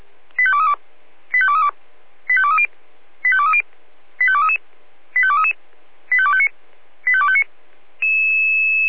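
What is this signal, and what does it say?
Five-tone selective call (ZVEI), heard as quick runs of five tones, tones of about 70 ms. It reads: 85232 85232 85230 85230 85233 85233 85259 85259
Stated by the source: ZVEI five-tone selective call 85252 received by SDR.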